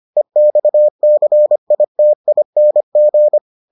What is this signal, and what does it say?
Morse code, heard as a keyed tone, about 25 wpm, 600 Hz. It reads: EXCITING